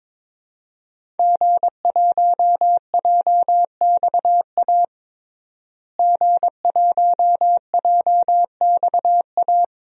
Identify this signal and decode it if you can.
Morse code, heard as a keyed tone, about 22 wpm, 700 Hz. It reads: G1JXA G1JXA
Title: DXCC-100 Call Signs - 5 Characters - Repeated 2x - 22wpm